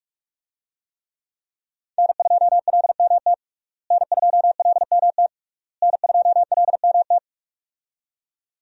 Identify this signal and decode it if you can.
Morse code, heard as a keyed tone, about 46 wpm, 700 Hz. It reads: N2LMT N2LMT N2LMT
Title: USA Call Signs - 5 Characters - Repeated 3x - 45wpm